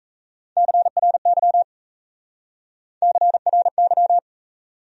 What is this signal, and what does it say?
Morse code, heard as a keyed tone, about 34 wpm, 700 Hz. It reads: CRY CRY
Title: Speed-Racing Top 500 Words 28wpm